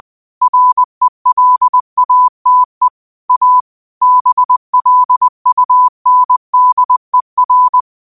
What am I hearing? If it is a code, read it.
Morse code, heard as a keyed tone, about 20 wpm, 1000 Hz. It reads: RELATE A BLUNDER
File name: 20wpm